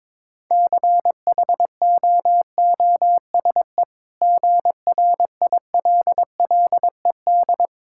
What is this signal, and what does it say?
Morse code, heard as a keyed tone, about 22 wpm, 690 Hz. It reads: CHOOSE GRILLED
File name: practice / 22wpm / Easy Sentences with Hints - 22wpm